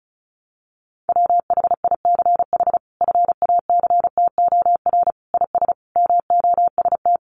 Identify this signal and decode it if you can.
Morse code, heard as a keyed tone, about 34 wpm, 700 Hz. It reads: WHICH FACTOR IS MOST
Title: Sentences from Top 1300 Words 35wpm